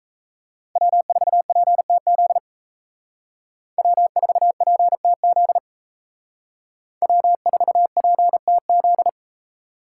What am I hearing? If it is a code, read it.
Morse code, heard as a keyed tone, about 33 wpm, 700 Hz. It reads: W4PTZ W4PTZ W4PTZ
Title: Speed Racing - USA Call Signs 28wpm